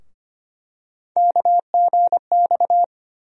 Morse code, heard as a keyed tone, about 25 wpm, 700 Hz. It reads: KGX
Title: USA Call Sign Postfixes - 3 Letters 25wpm